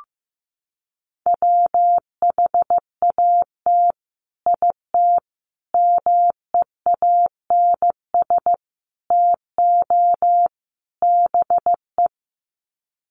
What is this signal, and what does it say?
Morse code, heard as a keyed tone, about 15 wpm, 700 Hz. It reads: WHAT IT MEANS TO BE
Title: Sets of 5 Words 15wpm